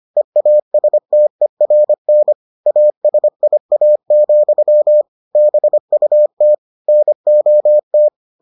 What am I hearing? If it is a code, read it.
Morse code, heard as a keyed tone, about 25 wpm, 600 Hz. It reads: EASTERN ASIA, BUT NOT